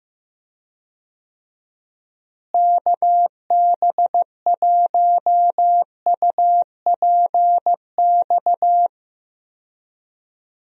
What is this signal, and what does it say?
Morse code, heard as a keyed tone, about 15 wpm, 700 Hz. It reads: KB1UPX
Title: USA Call Signs - 6 Characters - Repeated 3x - 15wpm